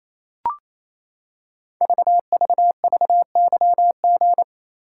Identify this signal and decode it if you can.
Morse code, heard as a keyed tone, about 27 wpm, 690 Hz. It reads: VVVYG
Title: Code Groups - 5 Letters - 28wpm